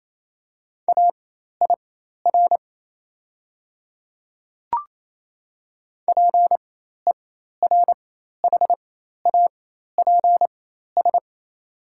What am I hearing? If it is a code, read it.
Morse code, heard as a keyed tone, about 28 wpm, spaced as for 13 wpm, 690 Hz. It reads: AIR PERHAPS
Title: 4x Character Spacing - Top 500 Words - 28wpm